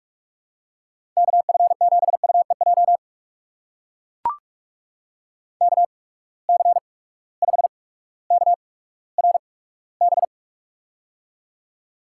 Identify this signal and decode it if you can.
Morse code, heard as a keyed tone, about 44 wpm, spaced as for 19 wpm, 700 Hz. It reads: KF7UEJ KC5KRB
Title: USA Call Signs - 6 Characters - 8x Character Spacing - 45wpm